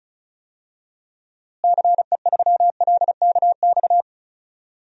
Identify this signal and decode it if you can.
Morse code, heard as a keyed tone, about 34 wpm, 700 Hz. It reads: CE3LKX
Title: DXCC-100 Call Signs - 6 Characters - Repeated 2x - 35wpm